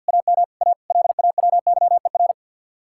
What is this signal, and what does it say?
Morse code, heard as a keyed tone, about 65 wpm, spaced as for 57 wpm, 700 Hz. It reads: AM A LAWYER
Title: Speed-Racing Common QSO Phrases Encore - 50wpm